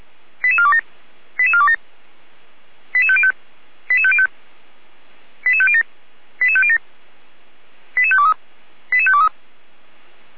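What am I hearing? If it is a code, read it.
Five-tone selective call (ZVEI), heard as quick runs of five tones, tones of about 70 ms. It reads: 88427 88427 88505 88505 88507 88507 88523 88523
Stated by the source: ZVEI five-tone call 88235 received by SDR, from audio server of Feuerwehr Orlamünde (fire brigade).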